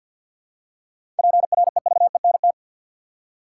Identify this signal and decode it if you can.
Morse code, heard as a keyed tone, about 53 wpm, 700 Hz. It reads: PREVENT
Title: Seven Letter Words - 50wpm